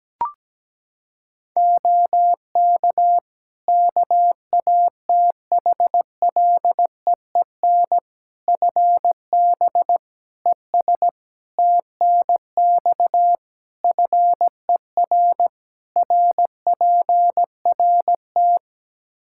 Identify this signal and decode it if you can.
Morse code, heard as a keyed tone, about 17 wpm, 700 Hz. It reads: OK KATHLEEN FB ES TNX FER RPRT
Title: Practice QSO - Exchange 2 - Info - 17wpm